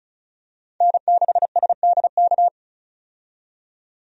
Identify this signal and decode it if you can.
Morse code, heard as a keyed tone, about 34 wpm, 700 Hz. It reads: N6SDK
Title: USA Call Signs - 5 Characters - 35wpm